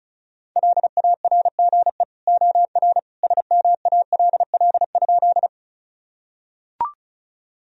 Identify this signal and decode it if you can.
Morse code, heard as a keyed tone, about 34 wpm, 700 Hz. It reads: LARGE OR SMALL?
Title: Sentences from Top 1000 Words - Review Part 1 - 35wpm